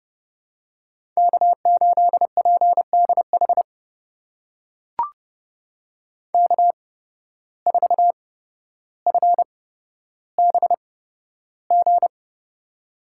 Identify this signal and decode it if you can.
Morse code, heard as a keyed tone, about 32 wpm, spaced as for 13 wpm, 700 Hz. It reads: K8PDH K4FBG
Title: USA Call Signs - 5 Characters - 8x Character Spacing - 30wpm